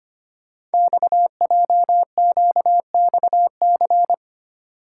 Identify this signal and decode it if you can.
Morse code, heard as a keyed tone, about 25 wpm, 700 Hz. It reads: XJQXC
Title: Code Groups - 5 Letters - 25wpm